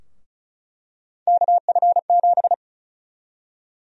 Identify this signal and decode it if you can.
Morse code, heard as a keyed tone, about 36 wpm, 700 Hz. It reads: KF7
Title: USA Call Sign Prefixes - 35wpm